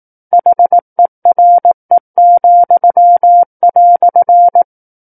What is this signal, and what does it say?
Morse code, heard as a keyed tone, about 18 wpm, 700 Hz. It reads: HERE,"
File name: 18 WPM